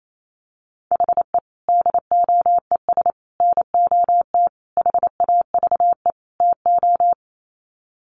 Morse code, heard as a keyed tone, about 28 wpm, 700 Hz. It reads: HE DOES NOT HAVE TO
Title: Sets of 5 Words Encore 28wpm